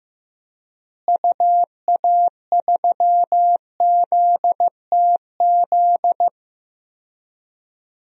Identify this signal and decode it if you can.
Morse code, heard as a keyed tone, about 15 wpm, 700 Hz. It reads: UA3ZTZ
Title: DXCC-100 Call Signs - 6 Characters - Repeated 2x - 15wpm